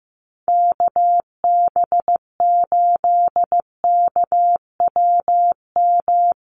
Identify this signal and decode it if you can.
Morse code, heard as a keyed tone, about 14 wpm, 700 Hz. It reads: KB8KWM
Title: USA Call Signs Repeated 3x 15wpm